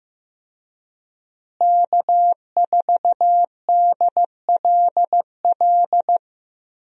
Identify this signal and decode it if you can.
Morse code, heard as a keyed tone, about 15 wpm, 700 Hz. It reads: K4DLL
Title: USA Call Signs - 5 Characters - Repeated 2x - 15wpm